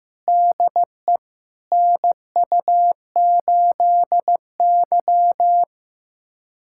Morse code, heard as a keyed tone, about 15 wpm, 700 Hz. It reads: DE NU8Y
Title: USA Callsign DE Callsign - 4 Characters - 15wpm